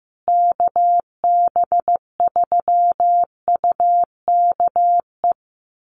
Morse code, heard as a keyed tone, about 15 wpm, 700 Hz. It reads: KB3UKE
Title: USA Call Signs Repeated 2x 15wpm